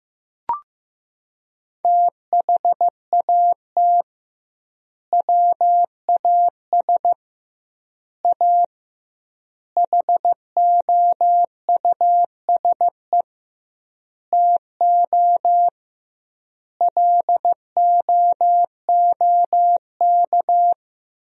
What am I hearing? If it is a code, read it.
Morse code, heard as a keyed tone, about 15 wpm, 700 Hz. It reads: THAT WAS A HOUSE TO LOOK
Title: 2x Word-Spacing - Sentences from Top 200 Words - 15wpm